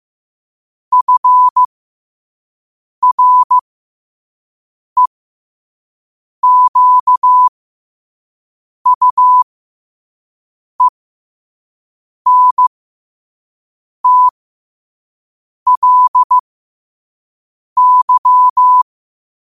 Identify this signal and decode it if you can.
Morse code, heard as a keyed tone, about 15 wpm, spaced as for 5 wpm, 1000 Hz. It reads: FREQUENTLY
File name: 5wpm